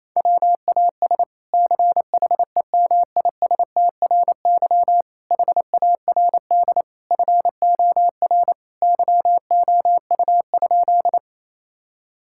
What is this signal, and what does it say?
Morse code, heard as a keyed tone, about 28 wpm, 700 Hz. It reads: WAS CHEMISTRY HARD FOR YOU?
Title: Sentences from Top 1000 Words - Review Part 6 - 28wpm